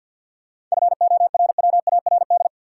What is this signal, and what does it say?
Morse code, heard as a keyed tone, about 53 wpm, 700 Hz. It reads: FORWARD
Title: Sentences from Top 1300 Words - New Word Drill - 50wpm